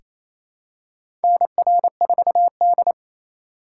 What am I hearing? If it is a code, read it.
Morse code, heard as a keyed tone, about 28 wpm, 700 Hz. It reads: NR4D
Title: USA Call Signs - 4 Characters - 2x Character Spacing - 28wpm